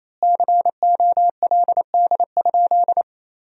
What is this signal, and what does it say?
Morse code, heard as a keyed tone, about 28 wpm, 700 Hz. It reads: COLD?
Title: 3x Word-Spacing - Sentences from Top 400 Words - 28wpm